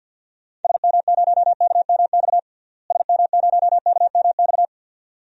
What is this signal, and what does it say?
Morse code, heard as a keyed tone, about 47 wpm, 700 Hz. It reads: SM0KMX SM0KMX
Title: DXCC-100 Call Signs - 6 Characters - Repeated 2x - 50wpm